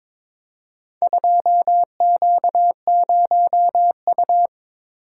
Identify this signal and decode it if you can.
Morse code, heard as a keyed tone, about 23 wpm, 700 Hz. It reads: 2Q0U
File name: practice / 22wpm / Four Letter-Number Combinations - 22wpm